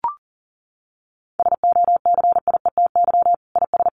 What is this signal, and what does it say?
Morse code, heard as a keyed tone, about 40 wpm, 700 Hz. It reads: SOCIETY IS
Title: Sentences from Top 1500 Words 40wpm